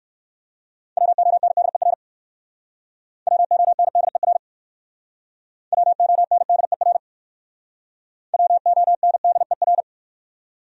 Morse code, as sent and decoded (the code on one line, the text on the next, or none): .-- --- -. -.. .-. / .-- --- -. -.. .-. / .-- --- -. -.. .-. / .-- --- -. -.. .-.
WONDER WONDER WONDER WONDER